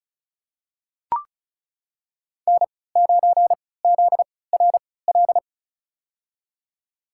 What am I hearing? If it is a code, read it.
Morse code, heard as a keyed tone, about 34 wpm, spaced as for 20 wpm, 700 Hz. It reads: N9ZRL
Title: USA Call Signs - 5 Characters - 3x Character Spacing - 35wpm